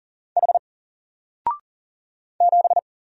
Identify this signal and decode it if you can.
Morse code, heard as a keyed tone, about 40 wpm, 700 Hz. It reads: H 7
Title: Single Letter-Number Alt 40wpm